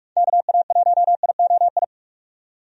Morse code, heard as a keyed tone, about 43 wpm, 700 Hz. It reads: KA1IOI
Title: USA Call Signs - 6 Characters - 45wpm